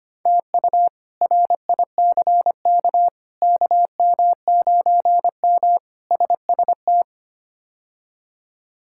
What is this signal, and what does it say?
Morse code, heard as a keyed tone, about 25 wpm, 700 Hz. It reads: TU RICK KM9M SST
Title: K1USN SST QSO Practice - 25wpm